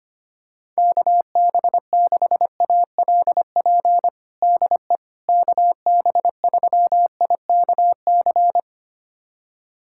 Morse code, heard as a keyed tone, about 25 wpm, 700 Hz. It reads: KB6ALP DE KB3IKC
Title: USA Callsign DE Callsign 25wpm